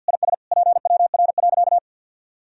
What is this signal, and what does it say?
Morse code, heard as a keyed tone, about 50 wpm, 700 Hz. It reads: IS PWR.
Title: Sentences from Top 500 Words and Abbreviations 50wpm